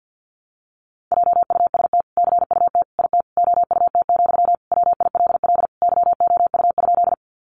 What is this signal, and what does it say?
Morse code, heard as a keyed tone, about 50 wpm, 700 Hz. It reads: JUST CUT IT OUT, WILL YOU?
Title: Sentences from Top 300 Words 50wpm